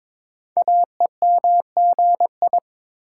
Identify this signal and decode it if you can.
Morse code, heard as a keyed tone, about 22 wpm, 700 Hz. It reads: AEMGI
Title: Code Groups - 5 Letters - 22wpm